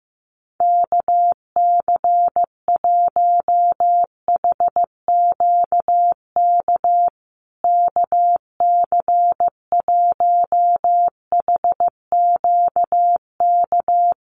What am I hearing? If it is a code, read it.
Morse code, heard as a keyed tone, about 15 wpm, 700 Hz. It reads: KC1HQK KC1HQK
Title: USA Call Signs Repeated 3x 15wpm